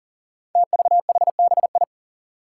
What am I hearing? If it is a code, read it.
Morse code, heard as a keyed tone, about 40 wpm, 700 Hz. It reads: TVHBI
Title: Five Challenging Letter-Number Combinations 40wpm